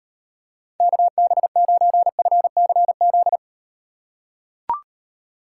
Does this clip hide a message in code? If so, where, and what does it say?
Yes: Morse code from 0.8 to 3.4 s, heard as a keyed tone, about 38 wpm, 700 Hz. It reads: KB9FCZ